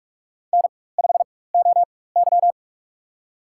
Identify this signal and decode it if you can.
Morse code, heard as a keyed tone, about 46 wpm, spaced as for 21 wpm, 700 Hz. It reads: N5OY
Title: USA Call Signs - 4 Characters - 4x Character Spacing - 45wpm